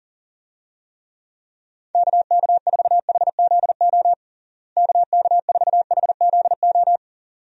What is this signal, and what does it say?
Morse code, heard as a keyed tone, about 40 wpm, 700 Hz. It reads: KK4HZO KK4HZO
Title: USA Call Signs - 6 Characters - Repeated 2x - 40wpm